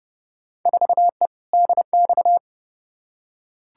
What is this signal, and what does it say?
Morse code, heard as a keyed tone, about 30 wpm, 700 Hz. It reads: 4E DX
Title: Field Day Exchanges Encore - 30wpm